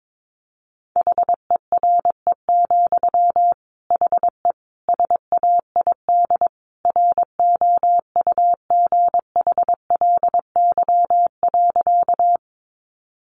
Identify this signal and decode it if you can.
Morse code, heard as a keyed tone, about 22 wpm, 700 Hz. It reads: HERE, HE SAID ROUGHLY.
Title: Sentences from Top 1500 Words 22wpm